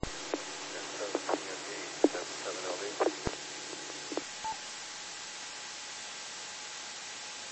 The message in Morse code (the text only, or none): none